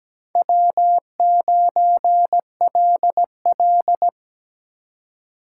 Morse code, heard as a keyed tone, about 20 wpm, 700 Hz. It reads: W9LL
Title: Speed Racing - USA 1x2 Call Signs 17wpm